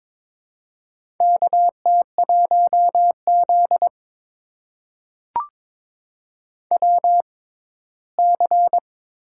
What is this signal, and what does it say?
Morse code, heard as a keyed tone, about 22 wpm, spaced as for 15 wpm, 700 Hz. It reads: KT1Z WC
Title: USA Call Signs - 4 Characters - 6x Character Spacing - 22wpm